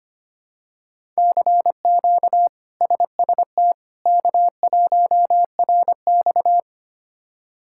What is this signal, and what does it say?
Morse code, heard as a keyed tone, about 25 wpm, 700 Hz. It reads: CQ SST K1RX
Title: K1USN SST QSO Practice - 25wpm